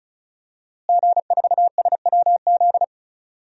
Morse code, heard as a keyed tone, about 36 wpm, 700 Hz. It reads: G4SWZ
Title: DXCC-100 Call Signs - 5 Characters - Repeated 3x - 35wpm